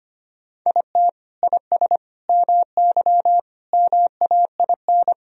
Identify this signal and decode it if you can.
Morse code, heard as a keyed tone, about 25 wpm, 700 Hz. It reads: IT IS MY MAIN
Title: Sentences from Top 1000 Words - Review Part 2 - 25wpm